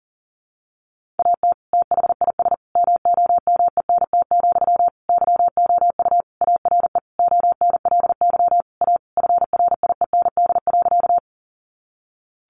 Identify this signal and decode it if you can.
Morse code, heard as a keyed tone, about 40 wpm, 700 Hz. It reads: AT THIS MOMENT, YOU ARE ONLY A FRIEND.
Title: Sentences from Top 1000 Words 40wpm